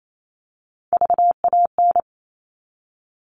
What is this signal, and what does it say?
Morse code, heard as a keyed tone, about 28 wpm, 700 Hz. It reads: VAN
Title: All Cut Numbers - 3-Digit 28wpm